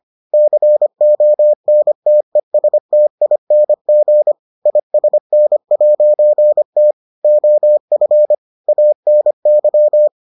Morse code, heard as a keyed tone, about 25 wpm, 600 Hz. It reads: CONTESTING ISN'T OF ANY